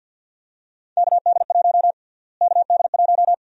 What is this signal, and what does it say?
Morse code, heard as a keyed tone, about 51 wpm, 700 Hz. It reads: KD1 KD1
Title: USA Call Sign Prefixes - Repeated 2x - 50wpm